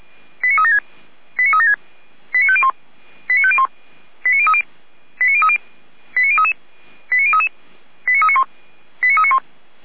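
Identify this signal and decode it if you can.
Five-tone selective call (ZVEI), heard as quick runs of five tones, tones of about 70 ms. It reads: 89376 89376 89501 89501 89030 89030 89033 89033 89391 89391
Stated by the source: ZVEI five-tone selective call received on SDR, recorded by feuerwehr-orlamuende fire brigade.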